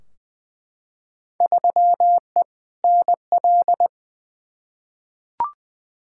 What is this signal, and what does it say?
Morse code, heard as a keyed tone, about 20 wpm, 700 Hz. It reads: 3E NL
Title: Field Day Exchanges Encore - 20wpm